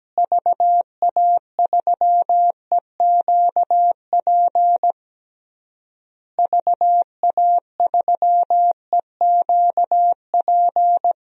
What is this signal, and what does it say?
Morse code, heard as a keyed tone, about 17 wpm, 700 Hz. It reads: VA3EQP VA3EQP
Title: DXCC-100 Call Signs - 6 Characters - Repeated 3x - 17wpm